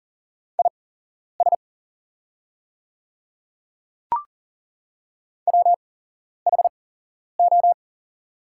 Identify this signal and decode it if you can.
Morse code, heard as a keyed tone, about 40 wpm, spaced as for 11 wpm, 700 Hz. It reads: IS WHO
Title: 8x Character Spacing - Top 100 Words - 40wpm